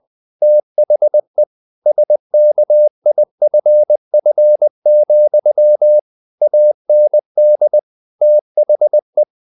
Morse code, heard as a keyed tone, about 20 wpm, 600 Hz. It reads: THE SKIFF, AND THE
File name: the old man and the sea / the old man and the sea part 1 20wpm